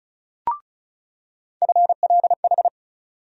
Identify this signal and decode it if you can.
Morse code, heard as a keyed tone, about 34 wpm, 700 Hz. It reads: FLH
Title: USA Call Sign Postfixes - 3 Letters 35wpm